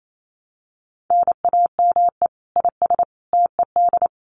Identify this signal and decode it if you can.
Morse code, heard as a keyed tone, about 28 wpm, 700 Hz. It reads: NAME IS TED
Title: Common QSO Phrases 28wpm